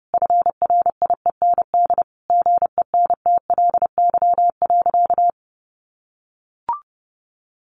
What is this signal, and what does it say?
Morse code, heard as a keyed tone, about 30 wpm, 700 Hz. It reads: FRIEND GENTLY.
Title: Sentences from Top 1500 Words 30wpm